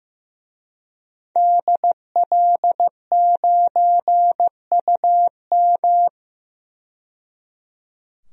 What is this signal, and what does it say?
Morse code, heard as a keyed tone, about 15 wpm, 700 Hz. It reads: DL9UM
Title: DXCC-100 Call Signs - 5 Characters - 15wpm